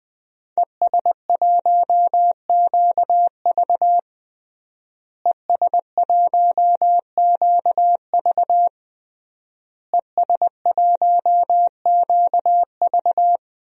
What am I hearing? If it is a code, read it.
Morse code, heard as a keyed tone, about 20 wpm, 700 Hz. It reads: ES1QV ES1QV ES1QV
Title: DXCC-100 Call Signs - 5 Characters - Repeated 3x - 20wpm